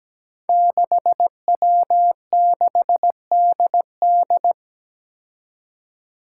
Morse code, heard as a keyed tone, about 17 wpm, 700 Hz. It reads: 6W6DD
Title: Five Challenging Letter-Number Combinations 17wpm